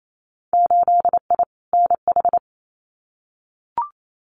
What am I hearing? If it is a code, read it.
Morse code, heard as a keyed tone, about 28 wpm, 700 Hz. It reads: 8I NH
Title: Winter Field Day Exchanges Encore 28wpm